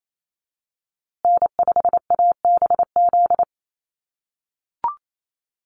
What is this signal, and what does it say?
Morse code, heard as a keyed tone, about 28 wpm, 700 Hz. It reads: N5ABZ
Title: USA Call Signs Repeated 2x 28wpm